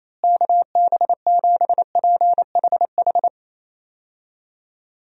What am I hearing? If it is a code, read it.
Morse code, heard as a keyed tone, about 28 wpm, 700 Hz. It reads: KB7PHH
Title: USA Call Signs - 6 Characters - Repeated 3x - 28wpm